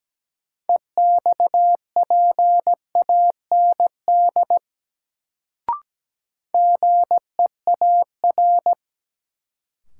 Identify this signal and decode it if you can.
Morse code, heard as a keyed tone, about 17 wpm, 700 Hz. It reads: EXPAND GEAR
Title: Sentences from Top 1400 Words - New Word Drill - 17wpm